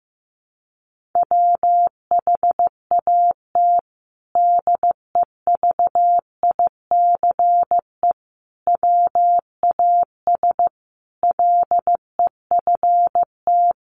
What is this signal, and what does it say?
Morse code, heard as a keyed tone, about 15 wpm, 700 Hz. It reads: WHAT DEVICE WAS LEFT
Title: Sentences from Top 1500 Words 15wpm